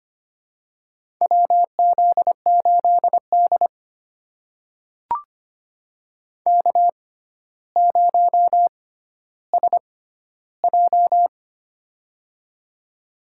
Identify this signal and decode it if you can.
Morse code, heard as a keyed tone, about 25 wpm, spaced as for 13 wpm, 700 Hz. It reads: WZ8D K0SJ
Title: USA Call Signs - 4 Characters - 6x Character Spacing - 25wpm